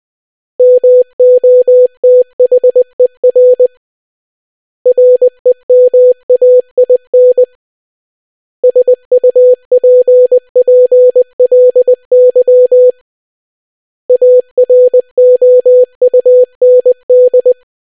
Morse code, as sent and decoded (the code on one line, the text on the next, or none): -- --- - .... .-. / .-. -- .- .. -. / ... ..- .--. .--. .-.. -.-- / .- .-. --- ..- -. -..
MOTHER REMAIN SUPPLY AROUND